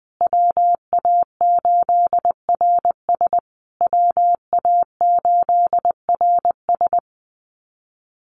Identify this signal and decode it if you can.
Morse code, heard as a keyed tone, about 20 wpm, 700 Hz. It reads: WA8RS WA8RS
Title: USA Call Signs Repeated 3x 20wpm